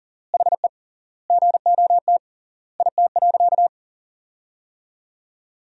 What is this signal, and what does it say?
Morse code, heard as a keyed tone, about 40 wpm, 700 Hz. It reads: HE GOT IT.
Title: 3x Word-Spacing - Sentences from Top 300 Words - 40wpm